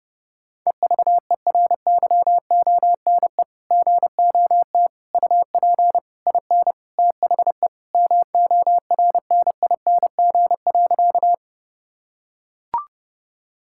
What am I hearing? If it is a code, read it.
Morse code, heard as a keyed tone, about 30 wpm, 700 Hz. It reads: EVERYONE GOT UP IN THE MORNING.
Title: Sentences from Top 1000 Words - Review Part 5 - 30wpm